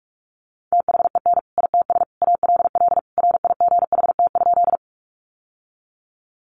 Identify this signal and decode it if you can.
Morse code, heard as a keyed tone, about 44 wpm, 700 Hz. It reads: THEN ITS ALL RIGHT?
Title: Sentences from Top 600 Words 45wpm